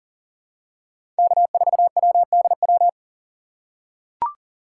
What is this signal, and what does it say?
Morse code, heard as a keyed tone, about 40 wpm, 700 Hz. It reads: K4WDW
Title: USA Call Signs - 5 Characters - 4x Character Spacing - 40wpm